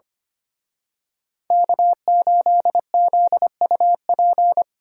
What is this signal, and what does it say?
Morse code, heard as a keyed tone, about 29 wpm, 700 Hz. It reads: K8ZUP
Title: Speed Racing - USA Call Signs 25wpm